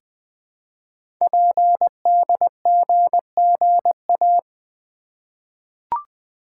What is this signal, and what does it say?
Morse code, heard as a keyed tone, about 20 wpm, 700 Hz. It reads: PDGGA
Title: Code Groups - 5 Letters - 20wpm